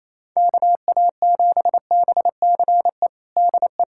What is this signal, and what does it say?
Morse code, heard as a keyed tone, about 27 wpm, 700 Hz. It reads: KA7BCE DE
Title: USA Callsign DE Callsign 28wpm